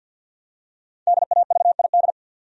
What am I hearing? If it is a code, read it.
Morse code, heard as a keyed tone, about 50 wpm, 700 Hz. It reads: DAVID
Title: USA Common Names - Max 5 Letters - 50wpm